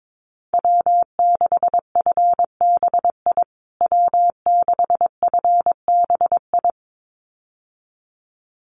Morse code, heard as a keyed tone, about 22 wpm, 700 Hz. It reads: W6FBI W6FBI
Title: USA Call Signs Repeated 2x 22wpm